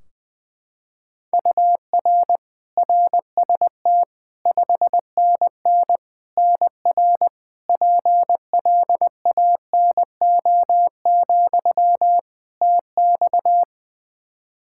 Morse code, heard as a keyed tone, about 20 wpm, 700 Hz. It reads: UR RST 5NN NR PLANO, TX
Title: Practice QSO - Exchange 1 - Beginning - 20wpm